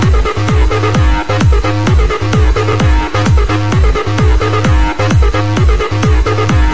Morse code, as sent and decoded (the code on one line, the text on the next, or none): none